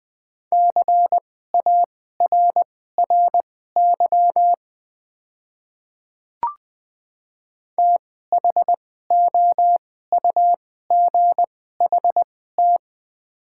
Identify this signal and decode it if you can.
Morse code, heard as a keyed tone, about 21 wpm, spaced as for 15 wpm, 700 Hz. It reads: CARRY THOUGHT
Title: 2x Character Spacing - Top 300 Words - 20wpm